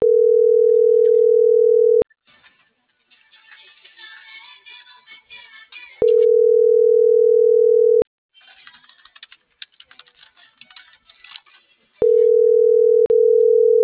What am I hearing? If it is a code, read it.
Ringback tone (North America), 440 + 480 Hz.
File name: CENTRAL NMP / out patricia.martineznmp - .36679